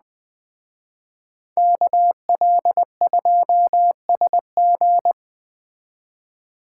Morse code, heard as a keyed tone, about 20 wpm, 700 Hz. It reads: KL2SG
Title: Speed Racing - USA Call Signs 17wpm